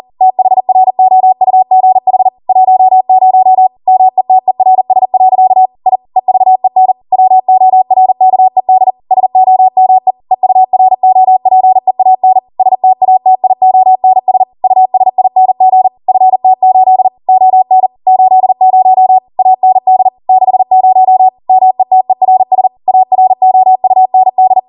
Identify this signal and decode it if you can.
Morse code, heard as a keyed tone, about 40 wpm, 750 Hz. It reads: THROUGH 10 METERS. I EVEN WORKED SOME EUROPEAN STATIONS USING FT8 ON 80 AND 60 METERS AROUND